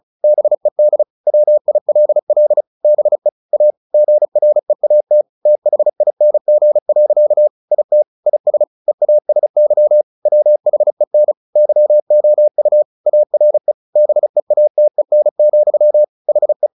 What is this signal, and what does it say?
Morse code, heard as a keyed tone, about 34 wpm, 600 Hz. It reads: BED WILL BE A GREAT THING. IT IS EASY WHEN YOU ARE BEATEN, HE
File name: the old man and the sea 2 35wpm